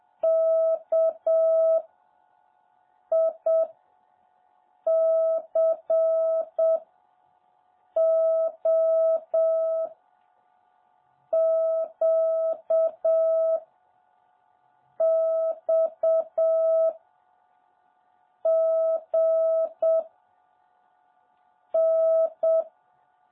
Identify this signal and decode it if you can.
Morse code, heard as a keyed tone, about 7 wpm, 650 Hz. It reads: K I C O Q X G N
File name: jh8cbh / 2021cw